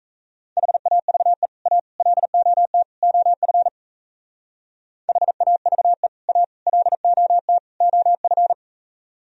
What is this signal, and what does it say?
Morse code, heard as a keyed tone, about 37 wpm, 700 Hz. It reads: HAVE A LOT OF HAVE A LOT OF